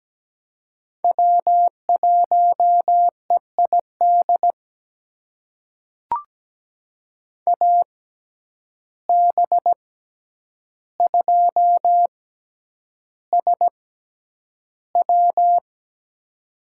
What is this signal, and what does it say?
Morse code, heard as a keyed tone, about 17 wpm, spaced as for 9 wpm, 700 Hz. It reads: W1EID AB2SW
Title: USA Call Signs - 6x Character Spacing - 17wpm